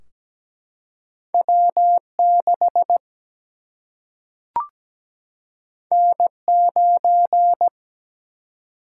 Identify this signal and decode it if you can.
Morse code, heard as a keyed tone, about 17 wpm, 700 Hz. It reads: W 6 N 9
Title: Single-Letter Single-Number 17wpm